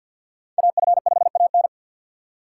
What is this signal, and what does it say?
Morse code, heard as a keyed tone, about 49 wpm, 700 Hz. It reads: AF5AN
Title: USA Call Signs - 5 Characters - 4x Character Spacing - 50wpm